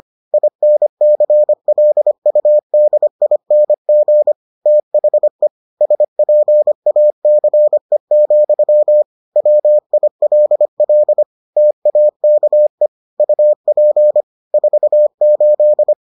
Morse code, heard as a keyed tone, about 25 wpm, 600 Hz. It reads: INCLUDING THE SPACE, WILL TAKE UP 48